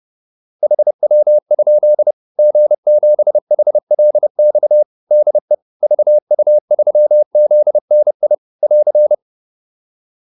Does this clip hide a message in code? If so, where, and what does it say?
Yes: Morse code from 0.6 to 9.1 s, heard as a keyed tone, about 30 wpm, 600 Hz. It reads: HW? G7HLX DE VU3ZNI <AR>